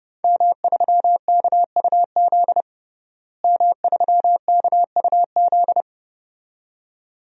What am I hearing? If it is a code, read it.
Morse code, heard as a keyed tone, about 30 wpm, 700 Hz. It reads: M3KUZ M3KUZ
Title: DXCC-100 Call Signs - 5 Characters - Repeated 2x - 30wpm